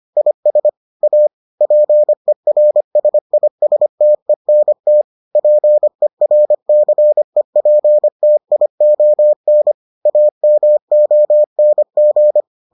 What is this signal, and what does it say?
Morse code, heard as a keyed tone, about 25 wpm, 600 Hz. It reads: IS A PERSISTENT PERCEPTION AMONG